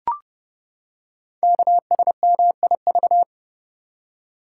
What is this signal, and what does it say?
Morse code, heard as a keyed tone, about 30 wpm, 700 Hz. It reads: KSMIV
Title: Code Groups - 5 Letters - 30wpm